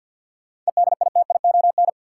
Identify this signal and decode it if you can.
Morse code, heard as a keyed tone, about 50 wpm, 700 Hz. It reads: EDITION
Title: Sets of 5 Words - New Word Drill - 50wpm